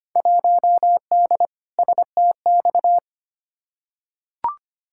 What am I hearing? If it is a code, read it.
Morse code, heard as a keyed tone, about 25 wpm, 700 Hz. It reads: 1D STX
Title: Field Day Exchanges - 25wpm